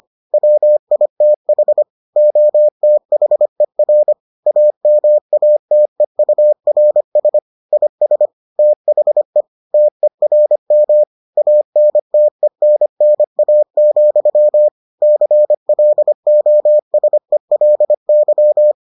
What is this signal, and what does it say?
Morse code, heard as a keyed tone, about 25 wpm, 600 Hz. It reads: WITH OTHER AMATEURS IS THE TERM ANTENNA, CLOSELY